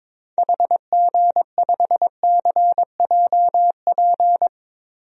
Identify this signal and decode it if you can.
Morse code, heard as a keyed tone, about 22 wpm, 700 Hz. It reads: HG5CJP
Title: DXCC-100 Call Signs - 6 Characters - 22wpm